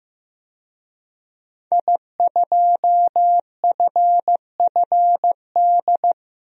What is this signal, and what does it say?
Morse code, heard as a keyed tone, about 15 wpm, 700 Hz. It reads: I2FFD
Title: DXCC-100 Call Signs - 5 Characters - 15wpm